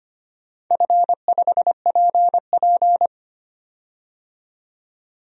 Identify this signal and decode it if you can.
Morse code, heard as a keyed tone, about 25 wpm, 700 Hz. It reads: F5PP
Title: DXCC-100 Call Signs - 4 Characters - 25wpm